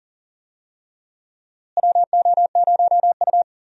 Morse code, heard as a keyed tone, about 40 wpm, 700 Hz. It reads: WO0U
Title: USA Call Signs - 4 Characters - Repeated 3x - 40wpm